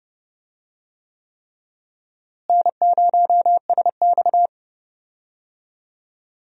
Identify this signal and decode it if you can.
Morse code, heard as a keyed tone, about 30 wpm, 700 Hz. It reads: N0SX
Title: USA Call Signs - 4 Characters - Repeated 3x - 30wpm